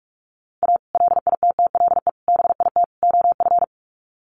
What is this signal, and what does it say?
Morse code, heard as a keyed tone, about 44 wpm, 700 Hz. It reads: A LITTLE BIT OF
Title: Sets of 4 Words 45wpm